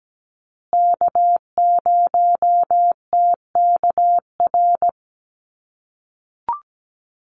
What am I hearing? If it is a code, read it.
Morse code, heard as a keyed tone, about 17 wpm, 700 Hz. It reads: K0TKR
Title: USA Call Signs Repeated 2x 17wpm